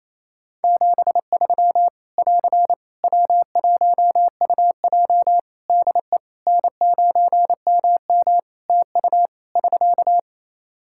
Morse code, heard as a keyed tone, about 28 wpm, 700 Hz. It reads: 73 <AR> W1UJ DE N9MM TU <SK>
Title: Practice QSO - Exchange 4 - Ending - 28wpm